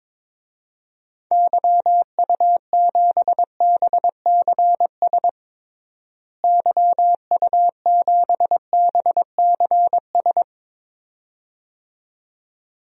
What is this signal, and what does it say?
Morse code, heard as a keyed tone, about 22 wpm, 700 Hz. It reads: YU7BCS YU7BCS
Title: DXCC-100 Call Signs - 6 Characters - Repeated 2x - 22wpm